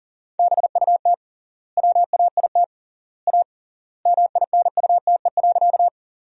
Morse code, heard as a keyed tone, about 40 wpm, 700 Hz. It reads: BUT WAIT A MINUTE.
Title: 3x Word-Spacing - Sentences from Top 500 Words - 40wpm